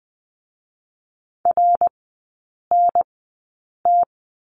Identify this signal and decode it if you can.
Morse code, heard as a keyed tone, about 20 wpm, spaced as for 8 wpm, 700 Hz. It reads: RNT
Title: 053 Lesson 10e - Letter r - Three Characters 20wpm